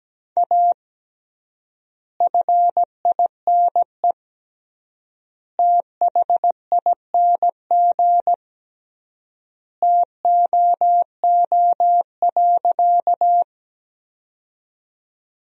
Morse code, heard as a keyed tone, about 17 wpm, 700 Hz. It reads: A FINE THING TOO.
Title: 3x Word-Spacing - Sentences from Top 500 Words - 17wpm